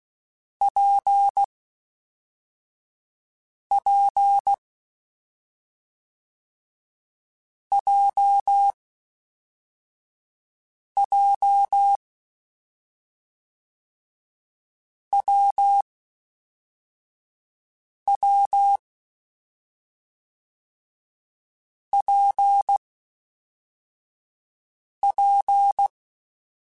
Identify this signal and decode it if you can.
Morse code, heard as a keyed tone, about 16 wpm, 800 Hz. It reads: P P J J W W P P
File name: Difficult Combos / Difficult Combos-mp3s / WPJ WPJ1 @5